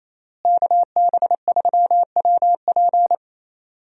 Morse code, heard as a keyed tone, about 28 wpm, 700 Hz. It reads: KB3WP
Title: USA Call Signs - 5 Characters - Repeated 2x - 28wpm